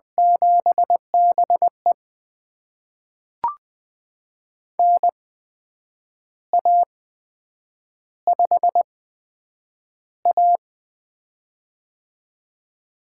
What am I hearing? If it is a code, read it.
Morse code, heard as a keyed tone, about 20 wpm, spaced as for 8 wpm, 700 Hz. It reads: 7BE NA5A
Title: USA Call Signs - 4 Characters - 8x Character Spacing - 20wpm